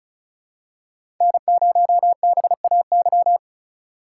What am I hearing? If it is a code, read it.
Morse code, heard as a keyed tone, about 35 wpm, 700 Hz. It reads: N0BAY
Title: USA Call Signs - 5 Characters - 2x Character Spacing - 35wpm